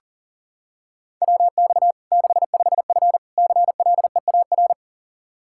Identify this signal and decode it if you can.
Morse code, heard as a keyed tone, about 40 wpm, 700 Hz. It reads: WX 65F CLEAR
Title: Practice QSO - Exchange 2 - Info - 40wpm